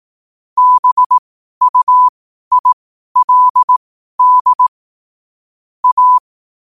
Morse code, heard as a keyed tone, about 18 wpm, spaced as for 12 wpm, 1000 Hz. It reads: BUILD A
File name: qotdfiles / 12wpm